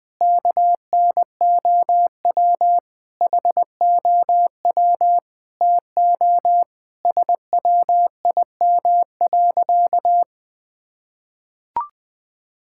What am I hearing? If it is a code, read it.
Morse code, heard as a keyed tone, about 20 wpm, 700 Hz. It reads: KNOW HOW TO SWIM.
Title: Sentences with 5-Word Phrases - 20wpm